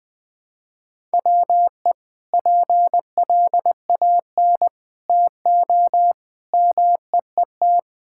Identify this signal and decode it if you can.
Morse code, heard as a keyed tone, about 20 wpm, 700 Hz. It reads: WE PLAN TO MEET
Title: Sentences with 2-Word Phrases - 20wpm